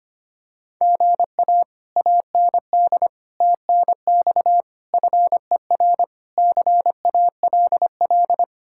Morse code, heard as a keyed tone, about 25 wpm, 700 Hz. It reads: GA AND TNX FER CALL
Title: Speed-Racing Common QSO Phrases - 25wpm